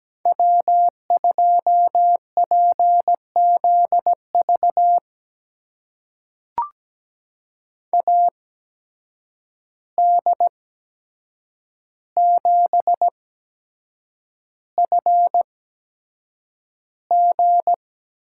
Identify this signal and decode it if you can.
Morse code, heard as a keyed tone, about 17 wpm, spaced as for 7 wpm, 700 Hz. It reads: W2PZV AD7FG
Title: USA Call Signs - 5 Characters - 8x Character Spacing - 17wpm